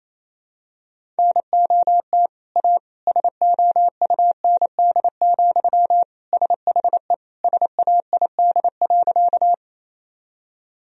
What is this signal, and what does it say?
Morse code, heard as a keyed tone, about 28 wpm, 700 Hz. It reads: NOT A SOUND, SHE SAID.
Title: Sentences from Top 1000 Words - Review Part 1 - 28wpm